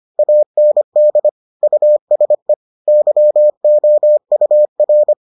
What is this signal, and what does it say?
Morse code, heard as a keyed tone, about 25 wpm, 600 Hz. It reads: AND USE YOUR